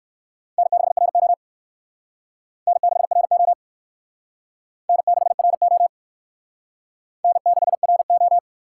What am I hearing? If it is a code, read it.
Morse code, heard as a keyed tone, about 58 wpm, 700 Hz. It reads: N6RO N6RO N6RO N6RO